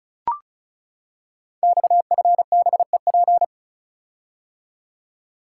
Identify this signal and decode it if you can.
Morse code, heard as a keyed tone, about 34 wpm, 700 Hz. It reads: XFBEP
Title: Code Groups - 5 Letter:Numbers - 35wpm